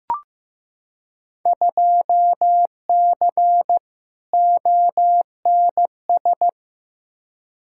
Morse code, heard as a keyed tone, about 15 wpm, 700 Hz. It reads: 2C ONS
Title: Field Day Exchanges Encore - 15wpm